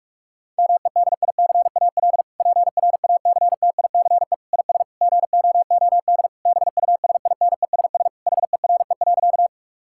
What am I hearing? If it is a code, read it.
Morse code, heard as a keyed tone, about 46 wpm, 700 Hz. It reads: MEDICAL PRACTICE IS GOOD BUSINESS HERE.
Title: Sentences from Top 1000 Words - Review Part 5 - 45wpm